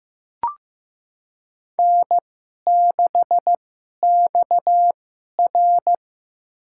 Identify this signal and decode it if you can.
Morse code, heard as a keyed tone, about 15 wpm, spaced as for 11 wpm, 700 Hz. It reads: N6XR
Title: USA Call Signs - 4 Characters - 2x Character Spacing - 15wpm